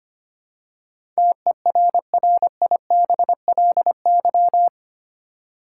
Morse code, heard as a keyed tone, about 25 wpm, 700 Hz. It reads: TERRIBLY
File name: Eight Letter Words - 25wpm